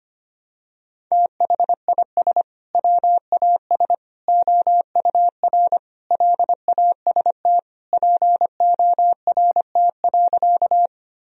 Morse code, heard as a keyed tone, about 25 wpm, 700 Hz. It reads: THIS WAS OUR LAST PORT.
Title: Sentences from Top 1000 Words - Review Part 1 - 25wpm